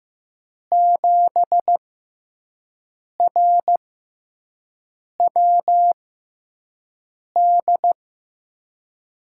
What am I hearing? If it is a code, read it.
Morse code, heard as a keyed tone, about 15 wpm, spaced as for 5 wpm, 700 Hz. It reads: 7RWD
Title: USA Call Signs - 6 Characters - 6x Character Spacing - 15wpm